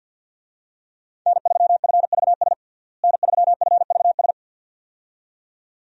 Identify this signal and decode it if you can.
Morse code, heard as a keyed tone, about 53 wpm, 700 Hz. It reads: N3FVS N3FVS
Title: USA Call Signs - 5 Characters - Repeated 2x - 50wpm